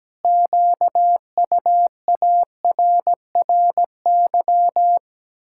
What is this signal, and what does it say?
Morse code, heard as a keyed tone, about 17 wpm, 700 Hz. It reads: QUARRY
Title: Pangram Challenge v2 - 17wpm